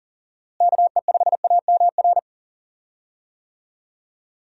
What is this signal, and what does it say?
Morse code, heard as a keyed tone, about 40 wpm, 700 Hz. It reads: KE5AMR
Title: USA Call Signs - 6 Characters - Repeated 3x - 40wpm